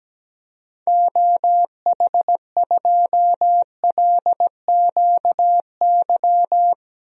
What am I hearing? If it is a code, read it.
Morse code, heard as a keyed tone, about 17 wpm, 700 Hz. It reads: OH2LQY
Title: DXCC-100 Call Signs - 6 Characters - 17wpm